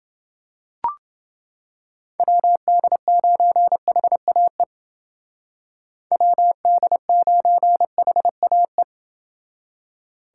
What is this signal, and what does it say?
Morse code, heard as a keyed tone, about 28 wpm, 700 Hz. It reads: WD9HAE WD9HAE